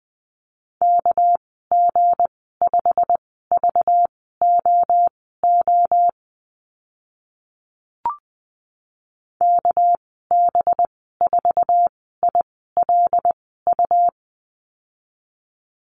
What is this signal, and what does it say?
Morse code, heard as a keyed tone, about 20 wpm, spaced as for 14 wpm, 700 Hz. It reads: KG5VOO KB4ILU
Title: 207 Lesson 33g - Letter j - Call Signs 20wpm